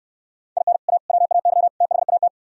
Morse code, heard as a keyed tone, about 65 wpm, 700 Hz. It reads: IN A WAY THAT